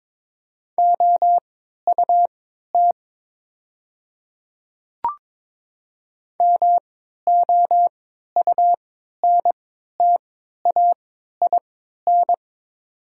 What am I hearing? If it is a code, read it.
Morse code, heard as a keyed tone, about 22 wpm, spaced as for 13 wpm, 700 Hz. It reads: OUT MOUNTAIN
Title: 3x Character Spacing - Top 500 Words - 22wpm